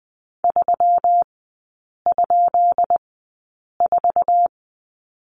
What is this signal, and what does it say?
Morse code, heard as a keyed tone, about 20 wpm, spaced as for 8 wpm, 690 Hz. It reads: 3?4
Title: 205 Lesson 33e - Letter j - Three Characters 20wpm